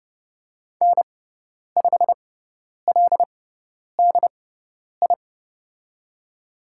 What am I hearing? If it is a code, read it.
Morse code, heard as a keyed tone, about 30 wpm, spaced as for 10 wpm, 700 Hz. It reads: N5LDI
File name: Lesson 17g - Letter u - Call Signs 30wpm